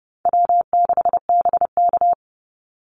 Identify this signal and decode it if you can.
Morse code, heard as a keyed tone, about 28 wpm, 700 Hz. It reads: W6BK
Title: Call Signs - 30wpm